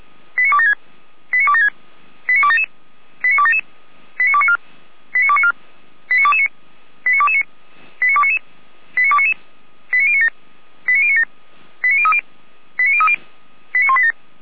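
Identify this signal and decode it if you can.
Five-tone selective call (ZVEI), heard as quick runs of five tones, tones of about 70 ms. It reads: 89276 89276 89277 89277 89294 89294 89229 89229 89299 89299 89097 89097 89030 89030 89187
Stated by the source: ZVEI five-tone call received by SDR.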